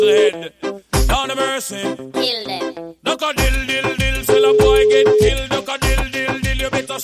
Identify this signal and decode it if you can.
Ringback tone (Europe, 425 Hz), with noise.